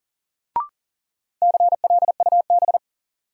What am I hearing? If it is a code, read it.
Morse code, heard as a keyed tone, about 40 wpm, 700 Hz. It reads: CLUB
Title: Sentences from Top 1200 Words - New Word Drill - 40wpm